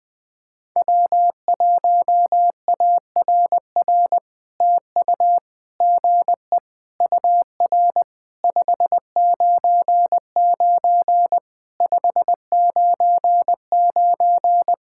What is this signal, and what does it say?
Morse code, heard as a keyed tone, about 19 wpm, 700 Hz. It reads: W1ARR TU GE UR 599 599
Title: POTA QSO Practice - 20wpm